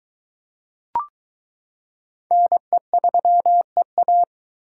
Morse code, heard as a keyed tone, about 23 wpm, 700 Hz. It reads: NE3EA